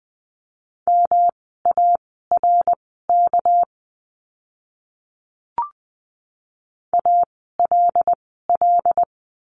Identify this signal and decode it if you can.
Morse code, heard as a keyed tone, about 20 wpm, spaced as for 14 wpm, 700 Hz. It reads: MARK ALL
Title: 199 Lesson 32f - Letter k - Words 20wpm